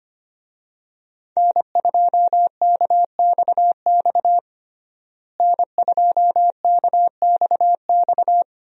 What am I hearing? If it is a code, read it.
Morse code, heard as a keyed tone, about 25 wpm, 700 Hz. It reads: N2KXX N2KXX
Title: USA Call Signs - 5 Characters - Repeated 3x - 25wpm